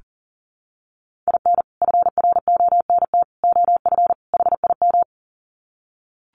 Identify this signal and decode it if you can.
Morse code, heard as a keyed tone, about 40 wpm, 700 Hz. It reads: IN FRONT OF HIM
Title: Sets of 4 Words 40wpm